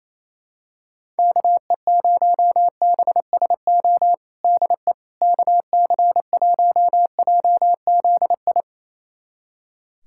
Morse code, heard as a keyed tone, about 28 wpm, 700 Hz. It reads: KE0BSO DE KC1JZI
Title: USA Callsign DE Callsign - 6 Characters - 28wpm